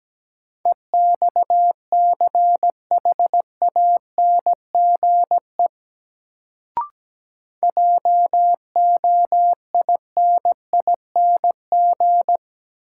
Morse code, heard as a keyed tone, about 17 wpm, 700 Hz. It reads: EXCHANGE JOINING
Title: Sets of 5 Words - New Word Drill - 17wpm